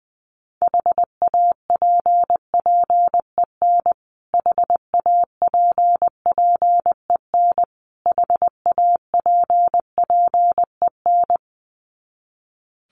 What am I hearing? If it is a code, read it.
Morse code, heard as a keyed tone, about 20 wpm, 700 Hz. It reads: HAPPEN HAPPEN HAPPEN
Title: Top 500 Words Repeated 20wpm